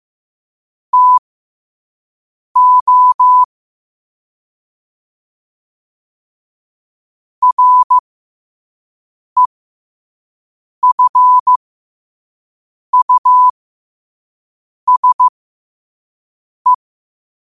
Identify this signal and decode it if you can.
Morse code, heard as a keyed tone, about 15 wpm, spaced as for 5 wpm, 1000 Hz. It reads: TO REFUSE
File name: 5wpm